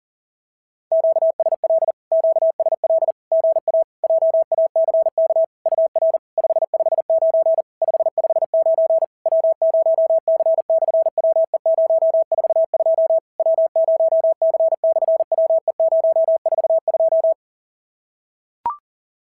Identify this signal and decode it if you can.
Morse code, heard as a keyed tone, about 40 wpm, 690 Hz. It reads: QSL QSL GA JACK UR 559 559 W0C/WE042 W0C/WE042
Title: SOTA QSO Practice 40wpm